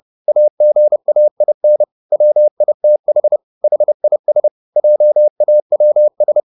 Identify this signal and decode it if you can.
Morse code, heard as a keyed tone, about 30 wpm, 600 Hz. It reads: AGAIN WITH HIS JAWS